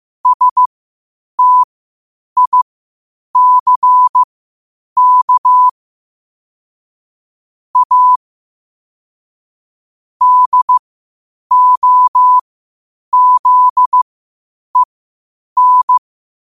Morse code, heard as a keyed tone, about 15 wpm, spaced as for 8 wpm, 1000 Hz. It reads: STICK A DOZEN
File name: qotdfiles / 8wpm